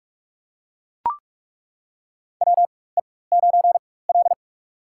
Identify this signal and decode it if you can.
Morse code, heard as a keyed tone, about 42 wpm, spaced as for 21 wpm, 700 Hz. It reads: WE9L
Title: USA Call Signs - 4x Character Spacing - 45wpm